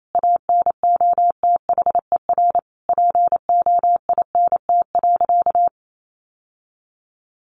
Morse code, heard as a keyed tone, about 28 wpm, 700 Hz. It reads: ANOTHER POINT.
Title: Sentences from Top 600 Words 28wpm